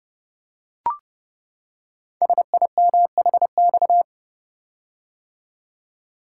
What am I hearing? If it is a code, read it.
Morse code, heard as a keyed tone, about 28 wpm, 700 Hz. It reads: SIMHX
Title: Code Groups - 5 Letters - 30wpm